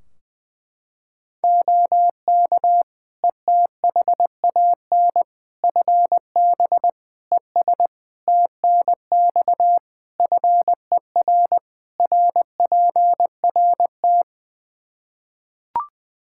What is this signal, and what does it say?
Morse code, heard as a keyed tone, about 21 wpm, 700 Hz. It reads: OK ETHAN FB ES TNX FER RPRT
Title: Practice QSO - Exchange 2 - Info - 20wpm